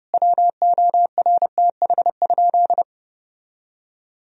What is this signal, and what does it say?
Morse code, heard as a keyed tone, about 32 wpm, 700 Hz. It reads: WORTH?
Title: Sentences from Top 1000 Words - Review Part 2 - 30wpm